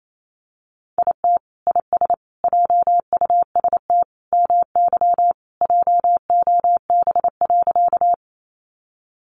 Morse code, heard as a keyed tone, about 28 wpm, 700 Hz. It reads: IT IS JUST MY JOB.
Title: Sentences from Top 600 Words 28wpm